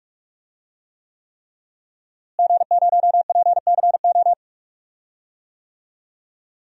Morse code, heard as a keyed tone, about 46 wpm, 700 Hz. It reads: G0PCO
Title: DXCC-100 Call Signs - 5 Characters - Repeated 3x - 45wpm